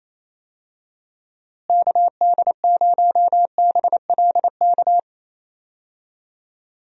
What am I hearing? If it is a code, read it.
Morse code, heard as a keyed tone, about 28 wpm, 700 Hz. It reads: KD0BLK
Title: USA Call Signs - 6 Characters - Repeated 3x - 28wpm